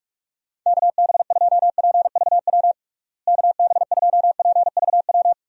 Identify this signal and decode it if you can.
Morse code, heard as a keyed tone, about 41 wpm, 700 Hz. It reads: KB2PVW KB2PVW
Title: USA Call Signs - 6 Characters - Repeated 2x - 45wpm